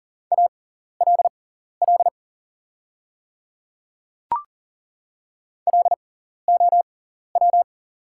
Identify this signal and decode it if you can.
Morse code, heard as a keyed tone, about 37 wpm, spaced as for 14 wpm, 700 Hz. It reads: ALL LOW
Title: 6x Character Spacing - Top 300 Words - 40wpm